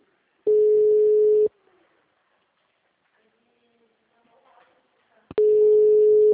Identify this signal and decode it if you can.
Ringback tone (Europe, 425 Hz), with speech.